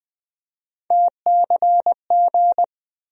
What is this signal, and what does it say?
Morse code, heard as a keyed tone, about 20 wpm, 700 Hz. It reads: TCG
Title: USA Call Sign Postfixes - 3 Letters 20wpm